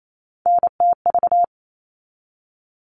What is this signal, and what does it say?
Morse code, heard as a keyed tone, about 28 wpm, 700 Hz. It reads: NTV
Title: All Cut Numbers - 3-Digit 28wpm